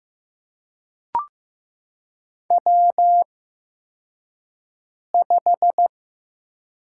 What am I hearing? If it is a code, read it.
Morse code, heard as a keyed tone, about 15 wpm, spaced as for 4 wpm, 700 Hz. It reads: W5
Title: USA Call Signs - 4 Characters - 8x Character Spacing - 15wpm